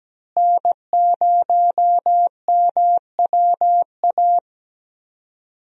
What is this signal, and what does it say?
Morse code, heard as a keyed tone, about 17 wpm, 700 Hz. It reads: N0MWA
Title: USA Call Signs - 5 Characters - Repeated 2x - 17wpm